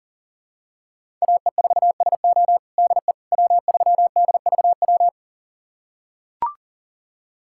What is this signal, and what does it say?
Morse code, heard as a keyed tone, about 40 wpm, 700 Hz. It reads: AE4SO DE W3DVW
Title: USA Callsign DE Callsign - 5 Characters - 40wpm